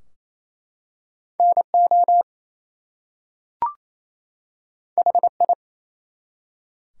Morse code, heard as a keyed tone, about 27 wpm, 700 Hz. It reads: NO HI
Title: Two Letter Words - 28wpm